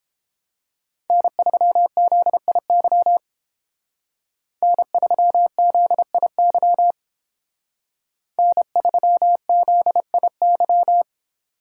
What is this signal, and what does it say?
Morse code, heard as a keyed tone, about 29 wpm, 700 Hz. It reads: N3ZIY N3ZIY N3ZIY